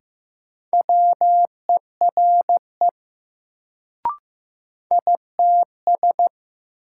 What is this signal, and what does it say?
Morse code, heard as a keyed tone, about 15 wpm, 700 Hz. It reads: WERE ITS
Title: Sets of 2 Words - New Word Drill - 15wpm